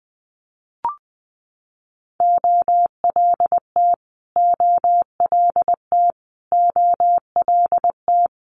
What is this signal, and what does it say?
Morse code, heard as a keyed tone, about 20 wpm, 700 Hz. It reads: OLT OLT OLT
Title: Multi-letter Phonemes Repeated 3x 20wpm